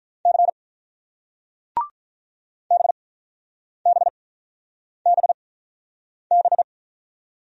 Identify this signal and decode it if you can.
Morse code, heard as a keyed tone, about 48 wpm, spaced as for 9 wpm, 700 Hz. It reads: C BBBB